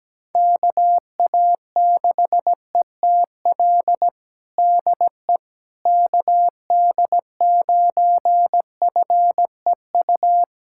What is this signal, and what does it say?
Morse code, heard as a keyed tone, about 17 wpm, 700 Hz. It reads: KA6ETL DE KD9FEU
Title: USA Callsign DE Callsign - 6 Characters - 17wpm